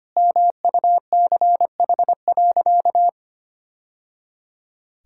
Morse code, heard as a keyed tone, about 25 wpm, 700 Hz. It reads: MUCH.
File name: practice / 25wpm / Sentences with 2-Word Phrases - 25wpm